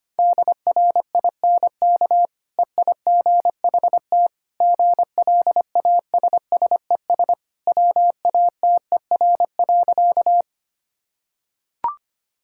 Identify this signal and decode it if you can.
Morse code, heard as a keyed tone, about 25 wpm, 700 Hz. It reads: DRINK EIGHT GLASSES WATER.
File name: Easy Sentences with Hints - 25wpm